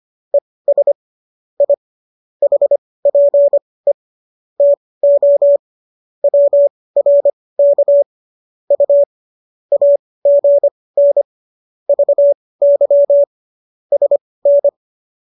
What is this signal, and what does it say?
Morse code, heard as a keyed tone, about 25 wpm, spaced as for 18 wpm, 570 Hz. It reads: ES I HPE TO WRK U AGN VY SN
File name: qso218 18